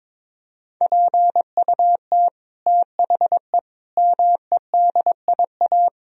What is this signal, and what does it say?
Morse code, heard as a keyed tone, about 22 wpm, 700 Hz. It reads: PUT THE MEDIA
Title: Sentences from Top 1000 Words - Review Part 4 - 22wpm